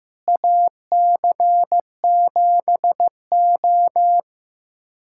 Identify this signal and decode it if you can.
Morse code, heard as a keyed tone, about 15 wpm, 700 Hz. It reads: AC7O
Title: USA Callsign DE Callsign - 4 Characters - 15wpm